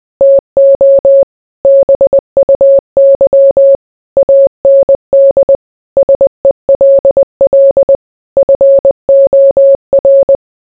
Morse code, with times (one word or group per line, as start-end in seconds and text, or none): 0.2-1.2 TO
1.6-3.8 BUY
4.2-5.6 AND
6.0-8.0 SELL
8.4-10.4 FOR